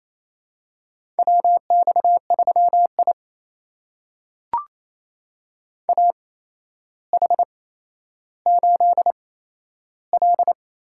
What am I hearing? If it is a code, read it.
Morse code, heard as a keyed tone, about 28 wpm, spaced as for 12 wpm, 700 Hz. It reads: WX3I AH8L
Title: USA Call Signs - 8x Character Spacing - 28wpm